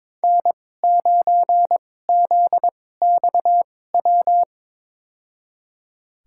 Morse code, heard as a keyed tone, about 22 wpm, spaced as for 16 wpm, 700 Hz. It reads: N9ZXW
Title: USA Call Signs - 5 Characters - 2x Character Spacing - 22wpm